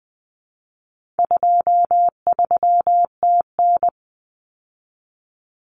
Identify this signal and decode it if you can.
Morse code, heard as a keyed tone, about 20 wpm, 700 Hz. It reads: 23TN
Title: Common Cut Numbers - 4-Digit 20wpm